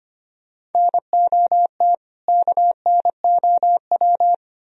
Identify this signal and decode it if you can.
Morse code, heard as a keyed tone, about 33 wpm, 700 Hz. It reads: NOT KNOW